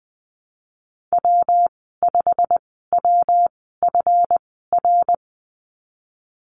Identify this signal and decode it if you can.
Morse code, heard as a keyed tone, about 20 wpm, spaced as for 15 wpm, 700 Hz. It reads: W5WFR
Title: 140 Lesson 23g - Letter f - Call Signs 20wpm